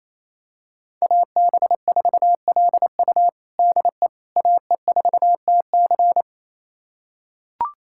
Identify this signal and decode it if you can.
Morse code, heard as a keyed tone, about 28 wpm, 700 Hz. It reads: AB4LU DE AE4TC
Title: USA Callsign DE Callsign - 5 Characters - 28wpm